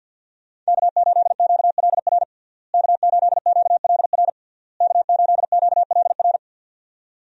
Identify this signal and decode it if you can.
Morse code, heard as a keyed tone, about 52 wpm, 700 Hz. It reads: K8QLR K8QLR K8QLR
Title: USA Call Signs - 5 Characters - Repeated 3x - 50wpm